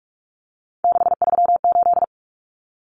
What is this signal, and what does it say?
Morse code, heard as a keyed tone, about 44 wpm, 700 Hz. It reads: 638
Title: Numbers - 3-Digit 45wpm